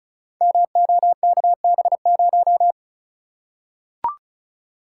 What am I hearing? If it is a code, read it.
Morse code, heard as a keyed tone, about 34 wpm, 700 Hz. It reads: MOKB0
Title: Code Groups - 5 Letter:Numbers - 35wpm